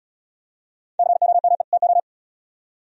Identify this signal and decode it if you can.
Morse code, heard as a keyed tone, about 76 wpm, 700 Hz. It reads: COME TO